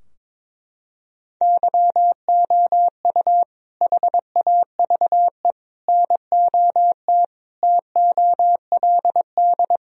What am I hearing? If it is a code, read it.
Morse code, heard as a keyed tone, about 22 wpm, 700 Hz. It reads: YOU HAVE NOT TOLD
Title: Sentences from Top 1000 Words - Review Part 2 - 22wpm